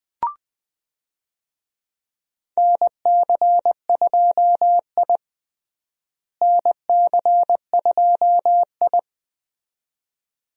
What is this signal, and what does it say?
Morse code, heard as a keyed tone, about 20 wpm, 700 Hz. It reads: NC2I NC2I
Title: USA Call Signs - 4 Characters - Repeated 2x - 20wpm